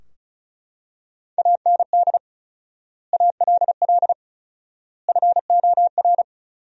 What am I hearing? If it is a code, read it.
Morse code, heard as a keyed tone, about 36 wpm, 700 Hz. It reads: AND ALL FOR
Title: 4x Word-Spacing - Sentences from Top 100 Words - 35wpm